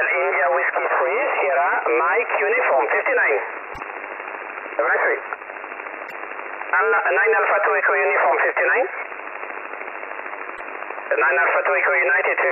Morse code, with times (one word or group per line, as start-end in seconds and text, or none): none